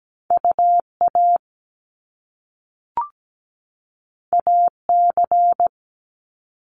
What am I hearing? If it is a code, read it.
Morse code, heard as a keyed tone, about 17 wpm, 700 Hz. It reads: UA AC
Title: Top Two Letter Combinations in Words 17wpm